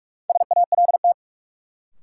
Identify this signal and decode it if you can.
Morse code, heard as a keyed tone, about 43 wpm, 700 Hz. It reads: SALT
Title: Sentences from Top 800 Words - New Word Drill - 45wpm